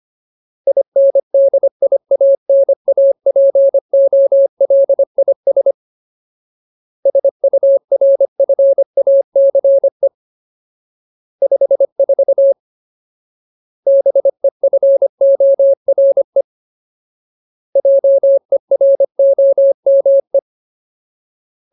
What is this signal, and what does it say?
Morse code, heard as a keyed tone, about 25 wpm, 550 Hz. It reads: INDIANAPOLIS SURFACE 54 BEFORE JEROME
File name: Ultimate Word List / 25WPM Ultimate